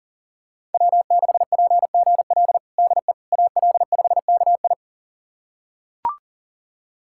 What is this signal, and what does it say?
Morse code, heard as a keyed tone, about 37 wpm, 700 Hz. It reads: W6PGL DE AL5KI
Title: USA Callsign DE Callsign - 5 Characters - 40wpm